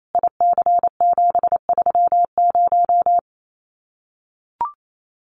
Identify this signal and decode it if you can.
Morse code, heard as a keyed tone, about 28 wpm, 700 Hz. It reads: IC730
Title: Common QSO Phrases 28wpm